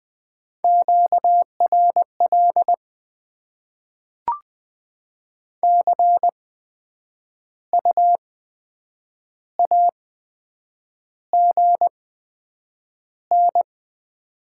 Morse code, heard as a keyed tone, about 20 wpm, spaced as for 8 wpm, 700 Hz. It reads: QRL CU AGN
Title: Top 20 QSO Elements - 8x Character Spacing - 20wpm